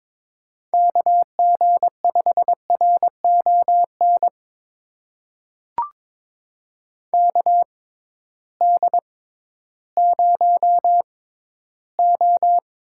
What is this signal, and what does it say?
Morse code, heard as a keyed tone, about 22 wpm, spaced as for 13 wpm, 700 Hz. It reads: KG5RON KD0O
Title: USA Call Signs - 6x Character Spacing - 22wpm